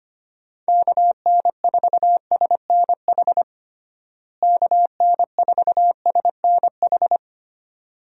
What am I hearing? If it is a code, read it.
Morse code, heard as a keyed tone, about 25 wpm, 700 Hz. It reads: KN4SNH KN4SNH
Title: USA Call Signs - 6 Characters - Repeated 2x - 25wpm